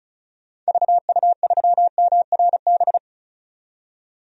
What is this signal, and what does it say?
Morse code, heard as a keyed tone, about 36 wpm, 700 Hz. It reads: VU3MRB
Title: DXCC-100 Call Signs - 6 Characters - 35wpm